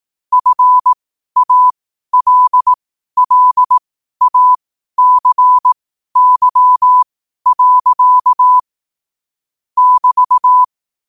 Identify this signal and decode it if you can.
Morse code, heard as a keyed tone, about 18 wpm, spaced as for 12 wpm, 1000 Hz. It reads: FALLACY. <BT>
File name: qotdfiles / 12wpm